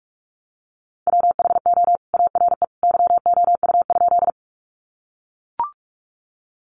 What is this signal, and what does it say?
Morse code, heard as a keyed tone, about 44 wpm, 700 Hz. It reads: WHO ARE YOU?
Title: Sentences from Top 100 Words 45wpm